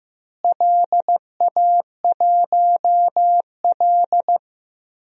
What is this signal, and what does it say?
Morse code, heard as a keyed tone, about 15 wpm, 700 Hz. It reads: LA1L
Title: DXCC-100 Call Signs - 4 Characters - Repeated 3x - 15wpm